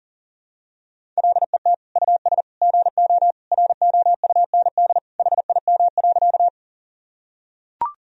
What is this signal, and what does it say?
Morse code, heard as a keyed tone, about 40 wpm, 700 Hz. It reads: LET US GO ROUND HIM.
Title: Sentences from Top 1000 Words - Review Part 1 - 40wpm